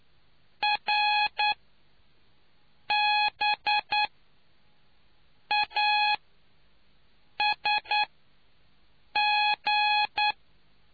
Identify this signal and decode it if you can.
Morse code, heard as a keyed tone, about 9 wpm, spaced as for 5 wpm, 840 Hz. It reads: RBASG